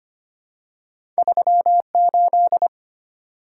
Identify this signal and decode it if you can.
Morse code, heard as a keyed tone, about 25 wpm, 700 Hz. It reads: 38
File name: Two Letter-Number Combinations - 25wpm